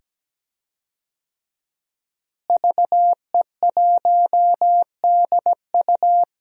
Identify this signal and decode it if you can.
Morse code, heard as a keyed tone, about 17 wpm, 700 Hz. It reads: VE1DU
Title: DXCC-100 Call Signs - 5 Characters - Repeated 2x - 17wpm